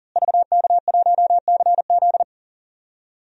Morse code, heard as a keyed tone, about 40 wpm, 700 Hz. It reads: VK1CZ
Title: DXCC-100 Call Signs - 5 Characters - Repeated 3x - 40wpm